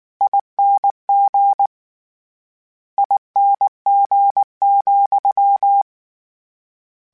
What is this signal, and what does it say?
Morse code, heard as a keyed tone, about 19 wpm, 800 Hz. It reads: ING ING,
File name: General / General-mp3s / ESE ING @18